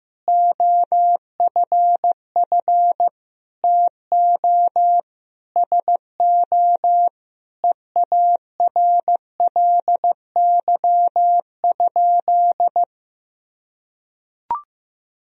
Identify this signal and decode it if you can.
Morse code, heard as a keyed tone, about 15 wpm, 700 Hz. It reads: OFF TO SO EARLY?
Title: Sentences from Top 400 Words and Abbreviations 15wpm